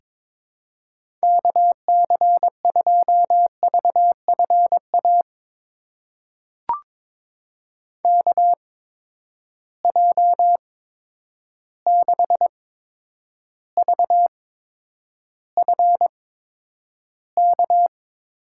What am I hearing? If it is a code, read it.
Morse code, heard as a keyed tone, about 22 wpm, spaced as for 9 wpm, 700 Hz. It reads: KC2VFA KJ6VFK
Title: USA Call Signs - 6 Characters - 8x Character Spacing - 22wpm